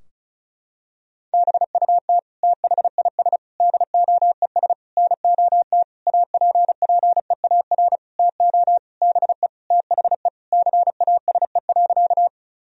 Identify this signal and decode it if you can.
Morse code, heard as a keyed tone, about 36 wpm, 700 Hz. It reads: BUT THIS DOES NOT APPEAR TO BE THE CASE.
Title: Sentences from Top 1000 Words - Review Part 2 - 35wpm